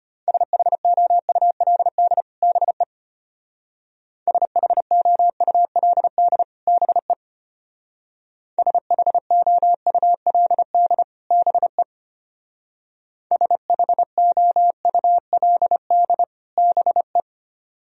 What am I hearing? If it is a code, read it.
Morse code, heard as a keyed tone, about 32 wpm, 700 Hz. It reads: SHOULD BE SHOULD BE SHOULD BE SHOULD BE